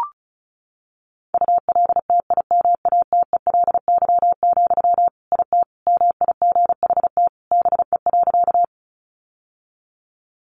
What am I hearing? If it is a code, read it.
Morse code, heard as a keyed tone, about 34 wpm, 700 Hz. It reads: ULTIMATELY, IT MIGHT BE.
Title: Sentences from Top 1200 Words 35wpm